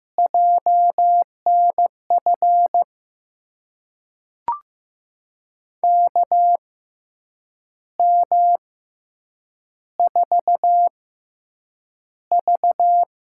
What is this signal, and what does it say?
Morse code, heard as a keyed tone, about 15 wpm, spaced as for 7 wpm, 700 Hz. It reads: JNF KM4V
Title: USA Call Signs - 6 Characters - 6x Character Spacing - 15wpm